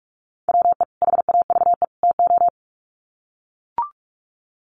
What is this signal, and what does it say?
Morse code, heard as a keyed tone, about 44 wpm, 700 Hz. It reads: WE HAVE TO
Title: Sets of 3 Words 45wpm